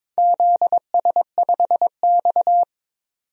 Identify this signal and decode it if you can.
Morse code, heard as a keyed tone, about 22 wpm, 700 Hz. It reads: ZS5X
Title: DXCC-100 Call Signs - 4 Characters - 22wpm